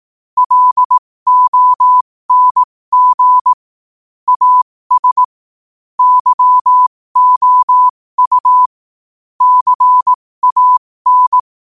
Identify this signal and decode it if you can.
Morse code, heard as a keyed tone, about 18 wpm, spaced as for 15 wpm, 1000 Hz. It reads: LONG AS YOU CAN
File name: qotdfiles / 15wpm